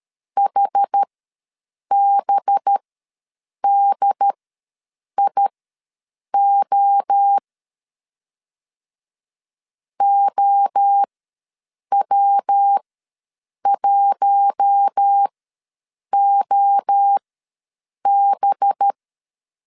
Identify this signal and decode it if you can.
Morse code, heard as a keyed tone, about 13 wpm, spaced as for 7 wpm, 800 Hz. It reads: HBDIO OW1OB
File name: LEZIONI / 5 LBVC / 5 LBV ws 12wpm 3015